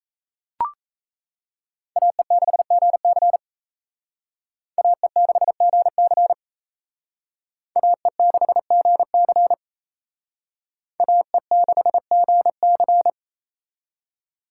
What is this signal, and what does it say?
Morse code, heard as a keyed tone, about 34 wpm, 700 Hz. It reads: AE6GC AE6GC AE6GC AE6GC
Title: Speed Racing - USA Call Signs 28wpm